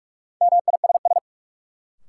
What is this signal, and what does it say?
Morse code, heard as a keyed tone, about 43 wpm, 700 Hz. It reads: MISS
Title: Sentences from Top 500 Words - New Word Drill - 45wpm